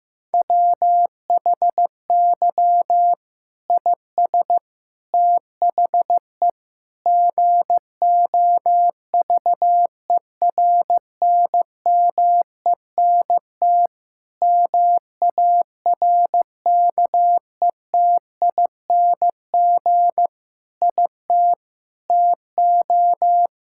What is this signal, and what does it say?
Morse code, heard as a keyed tone, about 15 wpm, 700 Hz. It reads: WHY IS THE GOVERNMENT MARKETING IT TO
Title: Sentences from Top 1000 Words - Review Part 5 - 15wpm